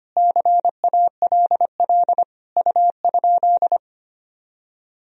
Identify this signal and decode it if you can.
Morse code, heard as a keyed tone, about 25 wpm, 700 Hz. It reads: CALL U?
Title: Sentences from Top 100 Words and Abbreviations 25wpm